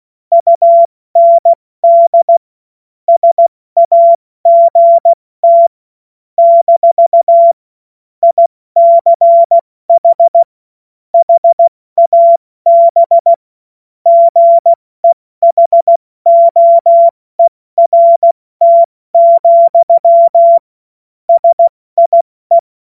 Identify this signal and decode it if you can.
Morse code, heard as a keyed tone, about 16 wpm, 680 Hz. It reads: UND SAGT - ICH HAB GEHOERT, SIE
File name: Morsekurs / Witze / Witz 05 März 2024 70 F80